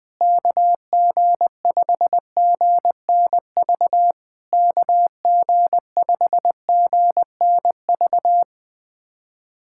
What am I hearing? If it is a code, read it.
Morse code, heard as a keyed tone, about 20 wpm, 700 Hz. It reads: KG5GNV KG5GNV
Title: SOTA QSO Practice 20wpm